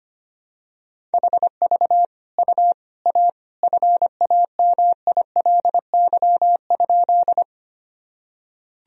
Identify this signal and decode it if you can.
Morse code, heard as a keyed tone, about 25 wpm, 700 Hz. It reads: HV U A FAMILY?
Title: Sentences from Top 400 Words and Abbreviations 25wpm